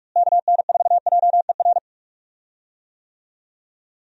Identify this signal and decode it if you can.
Morse code, heard as a keyed tone, about 45 wpm, 700 Hz. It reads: KN4JER
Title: USA Call Signs - 6 Characters - Repeated 2x - 45wpm